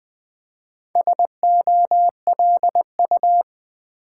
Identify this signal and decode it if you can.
Morse code, heard as a keyed tone, about 20 wpm, 700 Hz. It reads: SOLU
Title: Build A Word - 4 Letters At A Time - 8-Letter Words - 20wpm